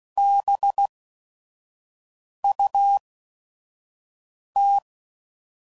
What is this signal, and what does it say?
Morse code, heard as a keyed tone, about 16 wpm, spaced as for 5 wpm, 800 Hz. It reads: BUT
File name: CW Code Groups @5WPM / QSOs / kc7haa